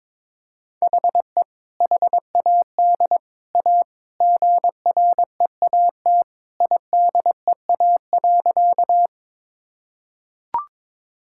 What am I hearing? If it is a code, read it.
Morse code, heard as a keyed tone, about 22 wpm, 700 Hz. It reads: HE HAD A GREAT IDEA.
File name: practice / 22wpm / Sentences with 3-Word Phrases - 22wpm